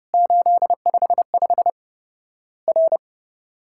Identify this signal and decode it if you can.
Morse code, heard as a keyed tone, about 30 wpm, 700 Hz. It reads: 855 R
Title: Worked All Europe DX Contest - QTC Exchange - 30wpm